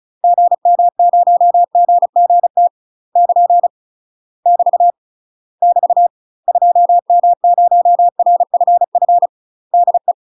Morse code, heard as a keyed tone, about 36 wpm, 700 Hz. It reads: GM0GGT <KN> <BT> <BT> 2M0RFF DE
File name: UK MASTER QSOs TRAINING / 35WPM / UK MASTER QSOs 0028